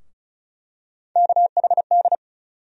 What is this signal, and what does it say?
Morse code, heard as a keyed tone, about 36 wpm, 700 Hz. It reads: KHD
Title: USA Call Sign Postfixes - 3 Letters 35wpm